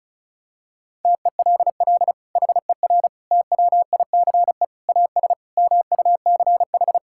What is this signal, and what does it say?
Morse code, heard as a keyed tone, about 36 wpm, 700 Hz. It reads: TELL HER TWICE AS MUCH
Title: Sentences from Top 1600 Words 35wpm